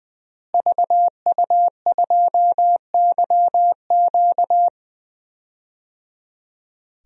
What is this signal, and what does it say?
Morse code, heard as a keyed tone, about 20 wpm, 700 Hz. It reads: VU2YQ
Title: DXCC-100 Call Signs - 5 Characters - Repeated 3x - 20wpm